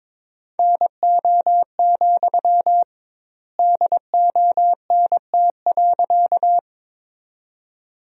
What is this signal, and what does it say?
Morse code, heard as a keyed tone, about 22 wpm, 700 Hz. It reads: NO, DONT.
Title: 2x Word-Spacing - Sentences from Top 300 Words - 22wpm